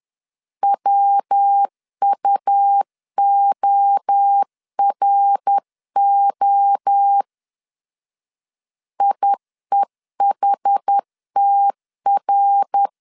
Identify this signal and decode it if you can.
Morse code, heard as a keyed tone, about 10 wpm, 800 Hz. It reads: WUORO IEHTR
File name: 10wpm